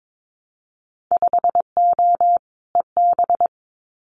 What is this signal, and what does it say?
Morse code, heard as a keyed tone, about 22 wpm, 700 Hz. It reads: 5O EB
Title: Winter Field Day Exchanges Encore 22wpm